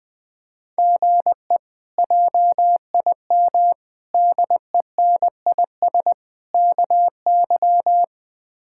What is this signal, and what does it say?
Morse code, heard as a keyed tone, about 20 wpm, 700 Hz. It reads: GE JIM DENIS KY
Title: K1USN SST QSO Practice - 20wpm